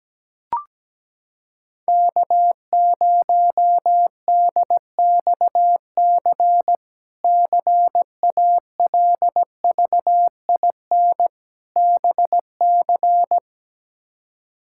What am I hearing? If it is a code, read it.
Morse code, heard as a keyed tone, about 17 wpm, 700 Hz. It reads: K0DXC CALVIN BC
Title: K1USN SST QSO Practice - 17wpm